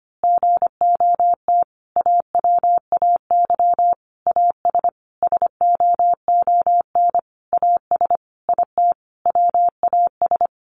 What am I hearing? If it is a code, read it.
Morse code, heard as a keyed tone, about 25 wpm, 700 Hz. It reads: GOT AWAY AS SOON AS IT WAS
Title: Sentences from Top 600 Words 25wpm